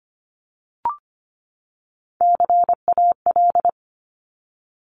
Morse code, heard as a keyed tone, about 24 wpm, 700 Hz. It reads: CAL
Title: Top Three Letter Combinations in Words 25wpm